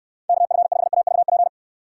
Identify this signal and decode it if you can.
Morse code, heard as a keyed tone, about 67 wpm, 700 Hz. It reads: KF5AVP